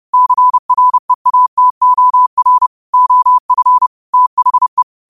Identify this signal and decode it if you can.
Morse code, heard as a keyed tone, about 30 wpm, 1000 Hz. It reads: CREATOR OF THE
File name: qotdfiles / 30wpm